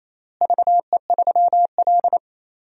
Morse code, heard as a keyed tone, about 28 wpm, 700 Hz. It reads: VE3L
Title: DXCC-100 Call Signs - 4 Characters - Repeated 3x - 28wpm